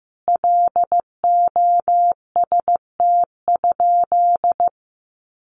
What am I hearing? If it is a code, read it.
Morse code, heard as a keyed tone, about 15 wpm, 700 Hz. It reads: LOST?
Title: Sentences from Top 1400 Words 15wpm